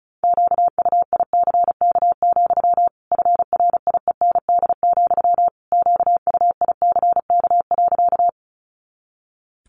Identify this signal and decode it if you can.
Morse code, heard as a keyed tone, about 34 wpm, 700 Hz. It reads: QUICK, FRIEND, QUICK.
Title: Sentences from Top 500 Words 35wpm